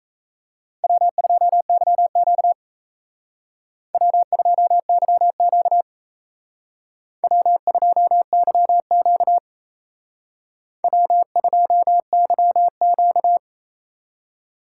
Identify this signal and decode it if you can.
Morse code, heard as a keyed tone, about 35 wpm, 700 Hz. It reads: W2YQ W2YQ W2YQ W2YQ
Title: Speed Racing - USA 1x2 Call Signs 28wpm